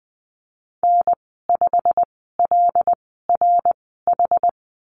Morse code, heard as a keyed tone, about 20 wpm, spaced as for 14 wpm, 700 Hz. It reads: N5LRH
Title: 088 Lesson 15g - Number 5 - Call Signs 20wpm